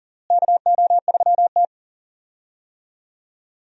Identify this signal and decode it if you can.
Morse code, heard as a keyed tone, about 39 wpm, 700 Hz. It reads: KO3T
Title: USA Call Signs - 4 Characters - Repeated 2x - 40wpm